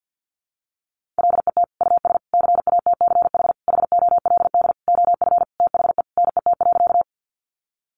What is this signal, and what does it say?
Morse code, heard as a keyed tone, about 50 wpm, 700 Hz. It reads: LET US CATCH HOLD OF THE NET.
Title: Sentences from Top 1500 Words 50wpm